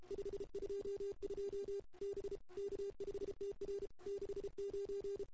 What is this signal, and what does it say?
Morse code, heard as a keyed tone, about 32 wpm, 410 Hz. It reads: 522 B K5TR 69
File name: CW / 2008-ss-cw-k5tr-op-n5rz -02:00